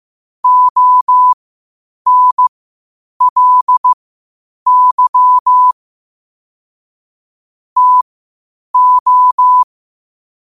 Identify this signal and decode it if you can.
Morse code, heard as a keyed tone, about 15 wpm, spaced as for 8 wpm, 1000 Hz. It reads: ONLY TO
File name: qotdfiles / 8wpm